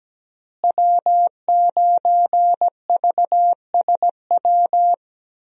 Morse code, heard as a keyed tone, about 17 wpm, 700 Hz. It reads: W9VSW
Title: USA Call Signs - 5 Characters - Repeated 3x - 17wpm